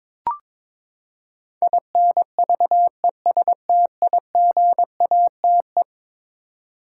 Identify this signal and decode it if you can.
Morse code, heard as a keyed tone, about 22 wpm, 700 Hz. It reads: INVESTIGATE
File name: practice / 22wpm / Eleven Letter Words - 22wpm